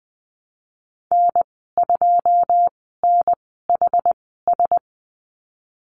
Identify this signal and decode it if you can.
Morse code, heard as a keyed tone, about 19 wpm, spaced as for 14 wpm, 700 Hz. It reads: N2NHS
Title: 179 Lesson 29g - Slash - Call Signs 20wpm